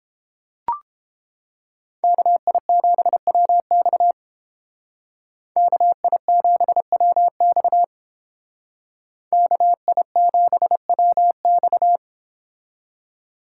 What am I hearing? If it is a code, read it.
Morse code, heard as a keyed tone, about 30 wpm, 700 Hz. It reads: KI7WX KI7WX KI7WX